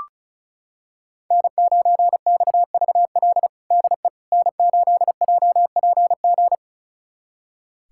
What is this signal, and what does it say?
Morse code, heard as a keyed tone, about 36 wpm, 700 Hz. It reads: N9XVL DE N8JPG
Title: USA Callsign DE Callsign - 5 Characters - 35wpm